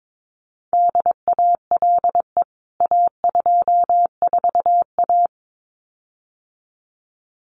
Morse code, heard as a keyed tone, about 22 wpm, 700 Hz. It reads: DALE A24A
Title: CWT Exchange 22wpm